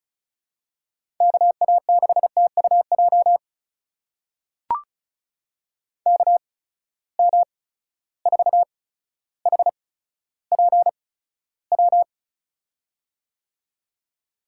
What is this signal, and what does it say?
Morse code, heard as a keyed tone, about 36 wpm, spaced as for 15 wpm, 700 Hz. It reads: KA6TUJ KM4HPW
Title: USA Call Signs - 6 Characters - 8x Character Spacing - 35wpm